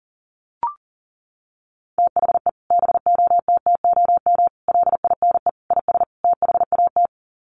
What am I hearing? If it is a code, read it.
Morse code, heard as a keyed tone, about 40 wpm, 700 Hz. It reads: THE BOTTOM LINE IS THAT
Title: Sets of 5 Words Encore 40wpm